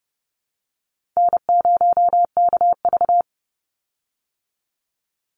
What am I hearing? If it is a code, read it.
Morse code, heard as a keyed tone, about 30 wpm, 700 Hz. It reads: N0KV
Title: Call Signs Repeated 3x - 30wpm